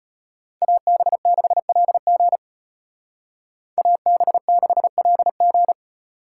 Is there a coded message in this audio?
Yes: Morse code from 0.6 to 5.7 s, heard as a keyed tone, about 32 wpm, 700 Hz. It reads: AB6LG AB6LG